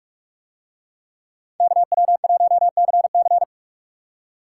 Morse code, heard as a keyed tone, about 41 wpm, 700 Hz. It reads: KW1CC